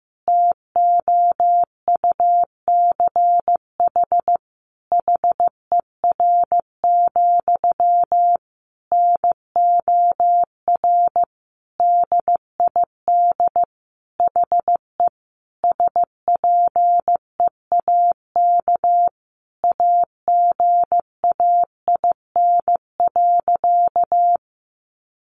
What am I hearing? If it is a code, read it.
Morse code, heard as a keyed tone, about 15 wpm, 700 Hz. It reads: TOUCH HER, NOR DID HE SPEAK AGAIN.
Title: Sentences from Top 1500 Words 15wpm